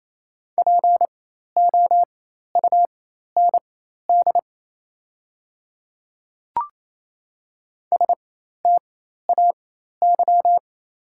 Morse code, heard as a keyed tone, about 28 wpm, spaced as for 13 wpm, 690 Hz. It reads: POUND STAY
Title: 4x Character Spacing - Top 500 Words - 28wpm